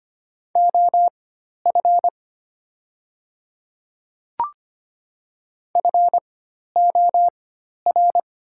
Morse code, heard as a keyed tone, about 25 wpm, spaced as for 12 wpm, 700 Hz. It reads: OF FOR
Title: 4x Character Spacing - Top 100 Words - 25wpm